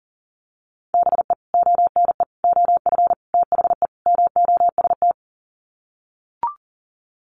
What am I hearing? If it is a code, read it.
Morse code, heard as a keyed tone, about 40 wpm, 700 Hz. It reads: BE ONE OF THE MOST
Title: Sets of 5 Words Encore 40wpm